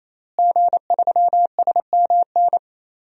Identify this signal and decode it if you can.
Morse code, heard as a keyed tone, about 28 wpm, 700 Hz. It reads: G3SMN
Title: DXCC-100 Call Signs - 5 Characters - Repeated 3x - 28wpm